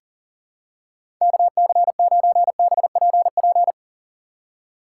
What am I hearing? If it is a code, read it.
Morse code, heard as a keyed tone, about 40 wpm, 700 Hz. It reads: KC9BPP
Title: USA Call Signs - 6 Characters - 3x Character Spacing - 40wpm